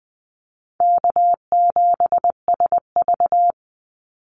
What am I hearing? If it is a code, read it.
Morse code, heard as a keyed tone, about 19 wpm, 700 Hz. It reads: K7SV
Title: Easy Call Signs - 20wpm